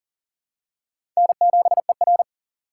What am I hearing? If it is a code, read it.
Morse code, heard as a keyed tone, about 40 wpm, 700 Hz. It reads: N7ER
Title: USA Call Signs - 6x Character Spacing - 40wpm